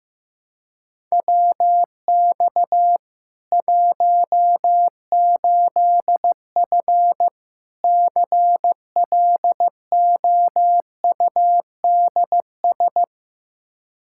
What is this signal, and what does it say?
Morse code, heard as a keyed tone, about 15 wpm, 700 Hz. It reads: WX 18F CLOUDS
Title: Practice QSO - Exchange 2 - Info - 15wpm